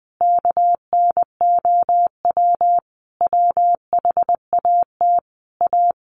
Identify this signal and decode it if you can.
Morse code, heard as a keyed tone, about 20 wpm, 700 Hz. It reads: KNOW WHAT A
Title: Sentences from Top 1500 Words 20wpm